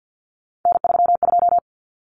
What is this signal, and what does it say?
Morse code, heard as a keyed tone, about 48 wpm, 700 Hz. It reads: N32
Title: Common Cut Numbers - 3-Digit 50wpm